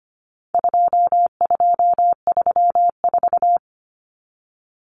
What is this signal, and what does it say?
Morse code, heard as a keyed tone, about 25 wpm, 700 Hz. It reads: 2234
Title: Common Cut Numbers - 4-Digit 25wpm